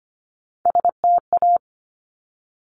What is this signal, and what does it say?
Morse code, heard as a keyed tone, about 25 wpm, 700 Hz. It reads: STA
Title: Top Three Letter Combinations in Words 25wpm